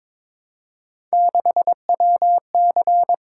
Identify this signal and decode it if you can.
Morse code, heard as a keyed tone, about 22 wpm, 700 Hz. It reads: 6WC
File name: practice / 22wpm / Three Letter-Number Combinations - 22wpm